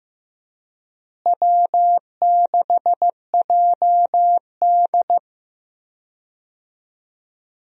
Morse code, heard as a keyed tone, about 15 wpm, 700 Hz. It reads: W6JD
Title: USA Call Signs - 4 Characters - Repeated 3x - 15wpm